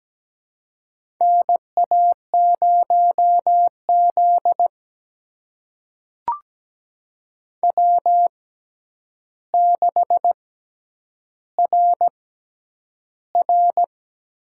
Morse code, heard as a keyed tone, about 17 wpm, spaced as for 9 wpm, 700 Hz. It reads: NA0Z W6RR
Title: USA Call Signs - 4 Characters - 6x Character Spacing - 17wpm